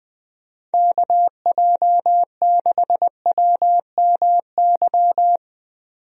Morse code, heard as a keyed tone, about 20 wpm, 700 Hz. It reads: KJ6WMY
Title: USA Call Signs - 6 Characters - 20wpm